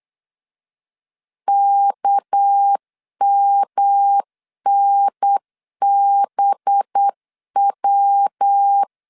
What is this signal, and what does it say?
Morse code, heard as a keyed tone, about 8 wpm, 800 Hz. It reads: KMNBW
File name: LEZIONI / 7 GF9Z / 7 G 8wpm 3118